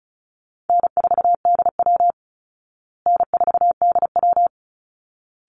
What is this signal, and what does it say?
Morse code, heard as a keyed tone, about 34 wpm, 700 Hz. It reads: N4DW N4DW
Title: Call Signs Repeated 3x - 35wpm